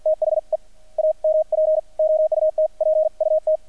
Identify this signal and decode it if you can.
Morse code, heard as a keyed tone, about 50 wpm, spaced as for 43 wpm, 630 Hz. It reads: THE AMP OUTPUT
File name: archived / CRSnet